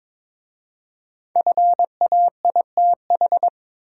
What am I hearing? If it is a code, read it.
Morse code, heard as a keyed tone, about 22 wpm, 690 Hz. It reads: FAITH
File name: 5 Letter Words - First 3 Letters Given - 22wpm